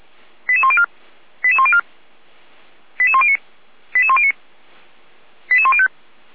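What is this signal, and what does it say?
Five-tone selective call (ZVEI), heard as quick runs of five tones, tones of about 70 ms. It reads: 88104 88104 88109 88109 88105
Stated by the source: ZVEI five-tone call received by SDR, from an archive run by Feuerwehr Orlamünde.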